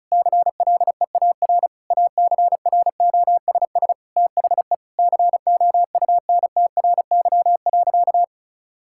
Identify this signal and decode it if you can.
Morse code, heard as a keyed tone, about 34 wpm, 700 Hz. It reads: CLEAR ACROSS THE COUNTRY.
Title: Sentences from Top 1000 Words - Review Part 5 - 35wpm